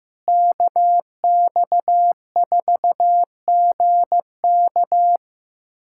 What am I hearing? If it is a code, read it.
Morse code, heard as a keyed tone, about 15 wpm, 700 Hz. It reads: KX4GK
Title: USA Callsign DE Callsign - 5 Characters - 15wpm